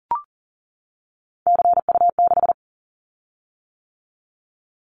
Morse code, heard as a keyed tone, about 40 wpm, 700 Hz. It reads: CU6
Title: Callsign Prefixes for Top 100 DXCC - 40wpm